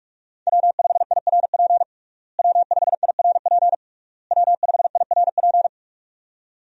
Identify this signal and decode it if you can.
Morse code, heard as a keyed tone, about 46 wpm, 700 Hz. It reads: W5IRP W5IRP W5IRP
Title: USA Call Signs - 5 Characters - Repeated 3x - 45wpm